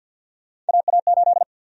Morse code, heard as a keyed tone, about 51 wpm, 700 Hz. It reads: AA8
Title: USA Call Sign Prefixes - 50wpm